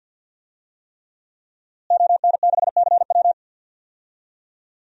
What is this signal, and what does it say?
Morse code, heard as a keyed tone, about 47 wpm, 700 Hz. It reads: ON6CW
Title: DXCC-100 Call Signs - 5 Characters - Repeated 3x - 50wpm